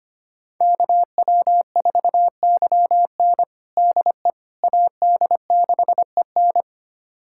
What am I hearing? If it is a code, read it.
Morse code, heard as a keyed tone, about 25 wpm, 700 Hz. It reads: KW4YN DE AD6EN
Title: USA Callsign DE Callsign - 5 Characters - 25wpm